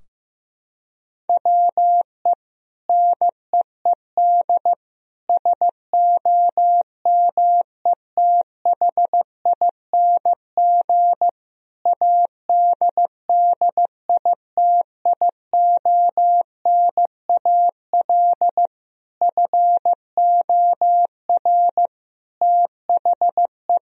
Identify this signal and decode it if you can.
Morse code, heard as a keyed tone, about 15 wpm, 700 Hz. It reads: WE NEED SOMETHING ADDITIONAL FOR THE
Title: Sentences from Top 1000 Words - Review Part 4 - 15wpm